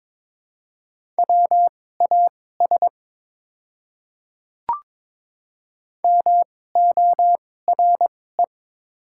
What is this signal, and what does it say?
Morse code, heard as a keyed tone, about 22 wpm, spaced as for 16 wpm, 700 Hz. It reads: WAS MORE
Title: 2x Character Spacing - Top 100 Words - 22wpm